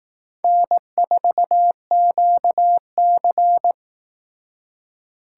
Morse code, heard as a keyed tone, about 18 wpm, 700 Hz. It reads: N4QC
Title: Speed Racing - USA 1x2 Call Signs 15wpm